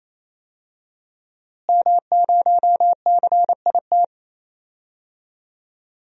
Morse code, heard as a keyed tone, about 28 wpm, 700 Hz. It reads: M0CIT
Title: DXCC-100 Call Signs - 5 Characters - Repeated 3x - 28wpm